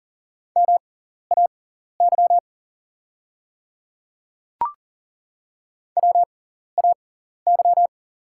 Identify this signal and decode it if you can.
Morse code, heard as a keyed tone, about 40 wpm, spaced as for 14 wpm, 690 Hz. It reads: MAY WAY
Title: 6x Character Spacing - Top 100 Words - 40wpm